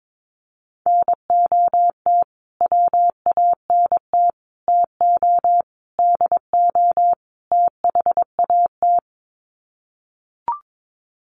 Morse code, heard as a keyed tone, about 22 wpm, 700 Hz. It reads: NOT WANT TO DO THAT
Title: Sets of 5 Words Encore 22wpm